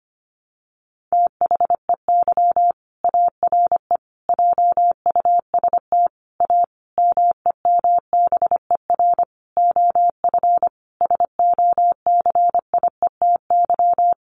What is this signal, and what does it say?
Morse code, heard as a keyed tone, about 25 wpm, 700 Hz. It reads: THEY ARE JUST A MEMBER OF SOCIETY
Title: Sentences from Top 1100 Words 25wpm